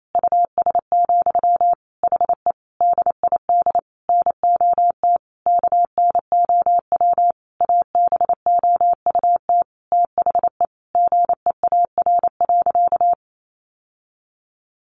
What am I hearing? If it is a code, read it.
Morse code, heard as a keyed tone, about 28 wpm, 700 Hz. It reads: US, HE DID NOT KNOW ABOUT THE GEAR.
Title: Sentences from Top 1400 Words 28wpm